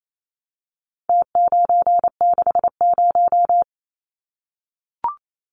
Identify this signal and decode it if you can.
Morse code, heard as a keyed tone, about 28 wpm, 700 Hz. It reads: T960
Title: Numbers - Contest Style - 4-Digit 28wpm